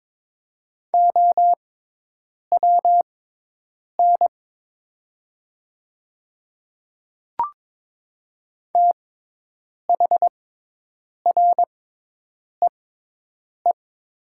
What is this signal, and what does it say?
Morse code, heard as a keyed tone, about 22 wpm, spaced as for 8 wpm, 700 Hz. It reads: OWN THREE
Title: 6x Character Spacing - Top 500 Words - 22wpm